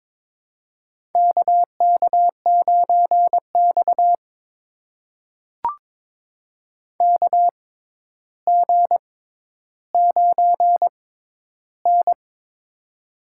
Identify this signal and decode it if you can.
Morse code, heard as a keyed tone, about 22 wpm, spaced as for 11 wpm, 700 Hz. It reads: KK9X KG9N
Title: USA Call Signs - 4 Characters - 6x Character Spacing - 22wpm